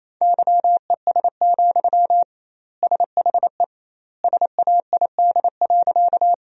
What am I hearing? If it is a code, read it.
Morse code, heard as a keyed tone, about 28 wpm, 700 Hz. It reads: YES, SHE SAID.
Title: 2x Word-Spacing - Sentences from Top 500 Words - 28wpm